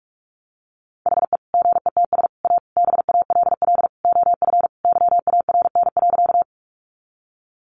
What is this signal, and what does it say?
Morse code, heard as a keyed tone, about 44 wpm, 700 Hz. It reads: HE GETS A BALL OF YARN.
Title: Additional Easy Phrases 45wpm